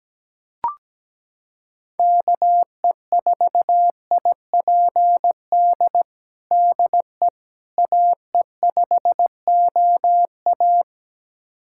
Morse code, heard as a keyed tone, about 17 wpm, 700 Hz. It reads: KE4IPD DE AE5OA
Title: USA Callsign DE Callsign 17wpm